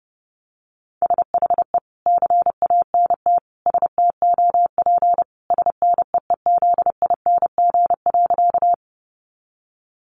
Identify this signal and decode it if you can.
Morse code, heard as a keyed tone, about 30 wpm, 700 Hz. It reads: SHE CANT STOP SNEEZING.
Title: Easy Phrases 30wpm